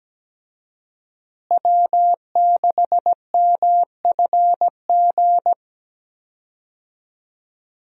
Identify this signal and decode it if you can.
Morse code, heard as a keyed tone, about 17 wpm, 700 Hz. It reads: W6MFG
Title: USA Call Signs - 5 Characters - Repeated 3x - 17wpm